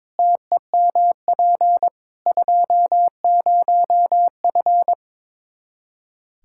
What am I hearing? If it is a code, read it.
Morse code, heard as a keyed tone, about 22 wpm, 700 Hz. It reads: TEMP 20F
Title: K1USN SST QSO Practice 22wpm